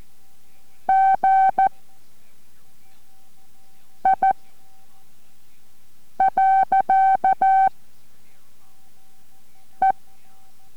Morse code, heard as a keyed tone, about 14 wpm, 770 Hz. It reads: G I . E